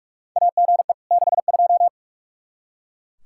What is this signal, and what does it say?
Morse code, heard as a keyed tone, about 42 wpm, 700 Hz. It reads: AGE 62
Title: K1USN SST QSO Practice 45wpm